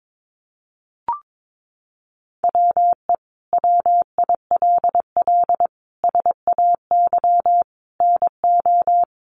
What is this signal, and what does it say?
Morse code, heard as a keyed tone, about 22 wpm, 700 Hz. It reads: WE WILL SAY NO
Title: Sentences from Top 200 Words 22wpm